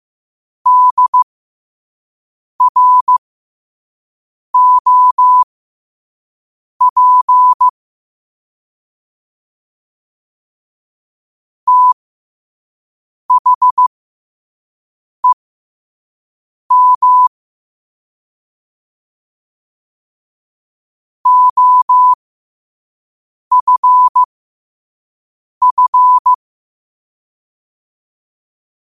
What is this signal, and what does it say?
Morse code, heard as a keyed tone, about 15 wpm, spaced as for 5 wpm, 1000 Hz. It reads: DROP THEM OFF
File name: qotdfiles / 5wpm